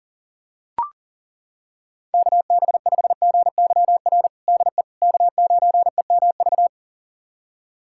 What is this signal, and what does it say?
Morse code, heard as a keyed tone, about 40 wpm, 700 Hz. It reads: KB5GYR DE K9EMV
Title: USA Callsign DE Callsign 40wpm